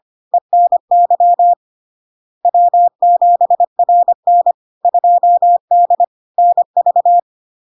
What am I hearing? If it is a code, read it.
Morse code, heard as a keyed tone, about 25 wpm, 700 Hz. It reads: ENY W7RN 2D NV
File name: Log-in-CW-25wpm-700-Hz